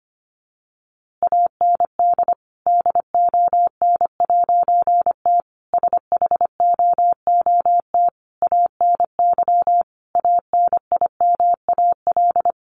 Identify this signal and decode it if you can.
Morse code, heard as a keyed tone, about 25 wpm, 700 Hz. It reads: AND DON'T SHOOT ANY ANIMAL
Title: A Sound Of Thunder 25wpm